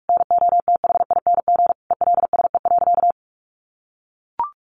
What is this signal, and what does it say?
Morse code, heard as a keyed tone, about 44 wpm, 700 Hz. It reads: NOTHING ELSE.
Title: Sentences from Top 800 Words 45wpm